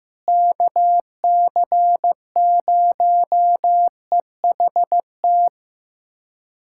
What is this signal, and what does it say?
Morse code, heard as a keyed tone, about 15 wpm, 700 Hz. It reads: KC0EHT
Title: USA Callsign DE Callsign 15wpm